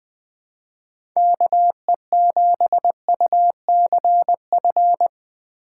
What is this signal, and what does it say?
Morse code, heard as a keyed tone, about 21 wpm, 700 Hz. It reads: KE7UCF
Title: USA Call Signs - 6 Characters - 4x Character Spacing - 20wpm